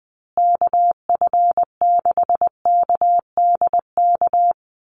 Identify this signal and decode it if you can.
Morse code, heard as a keyed tone, about 20 wpm, 700 Hz. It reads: KF6KDK
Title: USA Call Signs Repeated 3x 20wpm